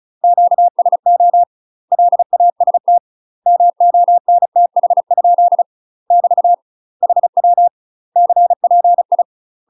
Morse code, heard as a keyed tone, about 34 wpm, 700 Hz. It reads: QSO LAST MONTH? <BT> HW CPI
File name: UK MASTER QSOs TRAINING / 35WPM / UK MASTER QSOs 0024